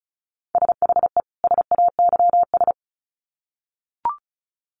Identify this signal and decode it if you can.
Morse code, heard as a keyed tone, about 34 wpm, 700 Hz. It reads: SHE SAYS
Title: Sets of 2 Words 35wpm